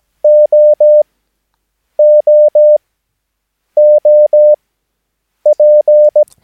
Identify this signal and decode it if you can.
Morse code, heard as a keyed tone, about 17 wpm, spaced as for 7 wpm, 600 Hz. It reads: OOOP
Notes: Övningsmaterial med alfabetet A till Ö samt siffror 0 till 9 enligt International/Continental Morse Code.